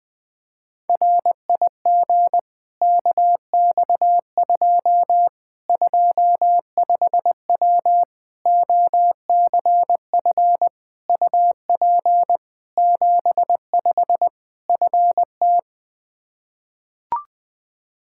Morse code, heard as a keyed tone, about 20 wpm, 700 Hz. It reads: RIG KX2 25W OCF UP 75 FT
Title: K1USN SST QSO Practice 20wpm